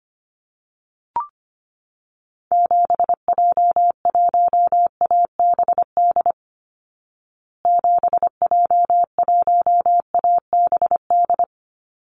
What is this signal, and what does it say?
Morse code, heard as a keyed tone, about 25 wpm, 700 Hz. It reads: 7J1ABD 7J1ABD
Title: Call Signs Repeated 3x - 25wpm